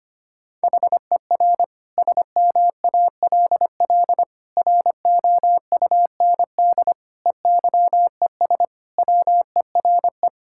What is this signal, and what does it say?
Morse code, heard as a keyed tone, about 25 wpm, 700 Hz. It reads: HER SMALL ROUND EYES WERE
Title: Sentences from Top 1000 Words - Review Part 6 - 25wpm